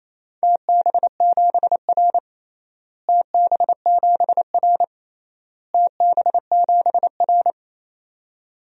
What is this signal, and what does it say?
Morse code, heard as a keyed tone, about 28 wpm, 700 Hz. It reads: TB7R TB7R TB7R
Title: DXCC-100 Call Signs - 4 Characters - Repeated 3x - 28wpm